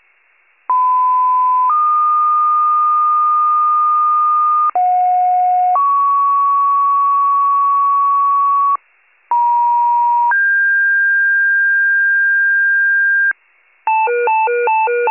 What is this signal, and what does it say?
Two-tone page, heard as a one-second tone, then a three-second tone, about 1010 then 1220 Hz, about 730 then 1120 Hz, about 950 then 1600 Hz. It is heard from 0.7 to 13.3 s.
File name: pager / 83 RESCUE DUTY / wcalert